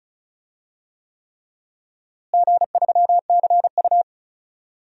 Morse code, heard as a keyed tone, about 34 wpm, 700 Hz. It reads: G3CU
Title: DXCC-100 Call Signs - 4 Characters - Repeated 3x - 35wpm